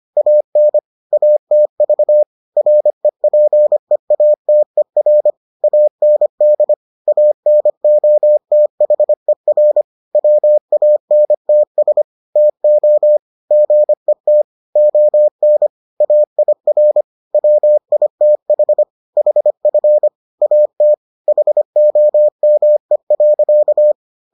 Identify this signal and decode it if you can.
Morse code, heard as a keyed tone, about 25 wpm, 600 Hz. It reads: AN ATV REPEATER AND ANOTHER WANTS TO GET ON AIR WITH HF AT HOME.